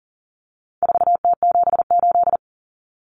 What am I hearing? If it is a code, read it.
Morse code, heard as a keyed tone, about 40 wpm, 700 Hz. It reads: 4T78
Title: Common Cut Numbers - 4-Digit 40wpm